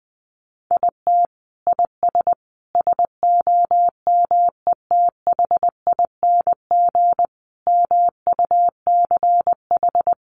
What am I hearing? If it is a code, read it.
Morse code, heard as a keyed tone, about 20 wpm, 700 Hz. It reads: IT IS SOMETHING MUCH
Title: Sentences from Top 1100 Words 20wpm